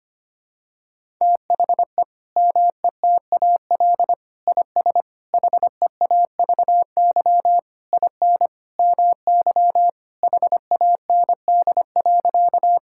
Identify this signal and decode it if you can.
Morse code, heard as a keyed tone, about 25 wpm, 700 Hz. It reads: THE METAL IS HEAVY IN MY HAND.
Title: Sentences from Top 1000 Words - Review Part 6 - 25wpm